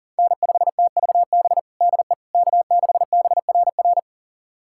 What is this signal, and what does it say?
Morse code, heard as a keyed tone, about 40 wpm, 700 Hz. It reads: N5TVB DE K6BRR
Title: USA Callsign DE Callsign - 5 Characters - 40wpm